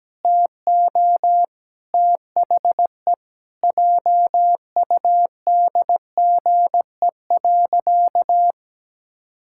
Morse code, heard as a keyed tone, about 17 wpm, 700 Hz. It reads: TO THE JUDGE.
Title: Pangram Challenge v2 - 17wpm